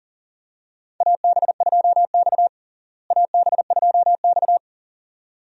Morse code, heard as a keyed tone, about 37 wpm, 700 Hz. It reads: AB2X AB2X
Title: USA Call Signs - 4 Characters - Repeated 2x - 40wpm